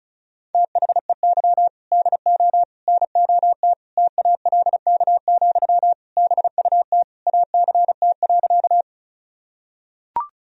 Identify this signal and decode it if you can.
Morse code, heard as a keyed tone, about 34 wpm, 700 Hz. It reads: THEY DO NOT TALK, BUT ACT.
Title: Sentences from Top 1000 Words - Review Part 1 - 35wpm